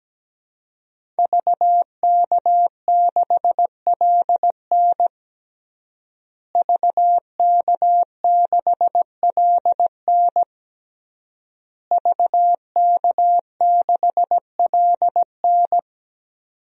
Morse code, heard as a keyed tone, about 17 wpm, 700 Hz. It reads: VK6LN VK6LN VK6LN
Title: DXCC-100 Call Signs - 5 Characters - Repeated 3x - 17wpm